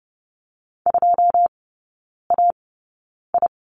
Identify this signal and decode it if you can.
Morse code, heard as a keyed tone, about 30 wpm, spaced as for 9 wpm, 700 Hz. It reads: 2AI
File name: 081 Lesson 14e - Number 2 - Three Characters 30wpm